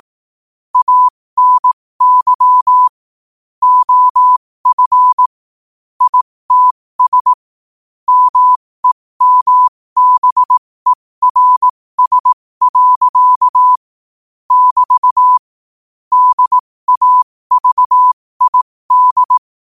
Morse code, heard as a keyed tone, about 18 wpm, spaced as for 15 wpm, 1000 Hz. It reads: ANY OF ITS MEMBERS. <BT> DAVID
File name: 15wpm